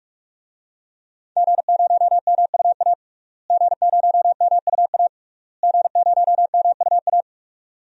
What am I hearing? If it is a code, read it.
Morse code, heard as a keyed tone, about 47 wpm, 700 Hz. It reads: G0MUA G0MUA G0MUA
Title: DXCC-100 Call Signs - 5 Characters - Repeated 3x - 45wpm